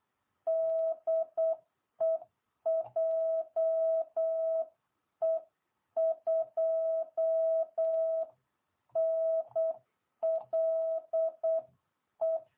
Morse code, heard as a keyed tone, about 8 wpm, 660 Hz. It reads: DE JE2NLE